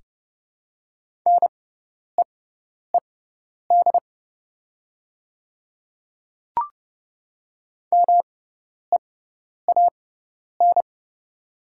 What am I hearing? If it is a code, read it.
Morse code, heard as a keyed tone, about 30 wpm, spaced as for 10 wpm, 700 Hz. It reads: NEED MEAN
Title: 6x Character Spacing - Top 200 Words - 30wpm